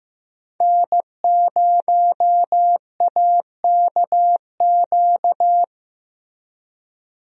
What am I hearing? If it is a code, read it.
Morse code, heard as a keyed tone, about 15 wpm, 700 Hz. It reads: N0AKQ
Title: USA Call Signs - 5 Characters - Repeated 3x - 15wpm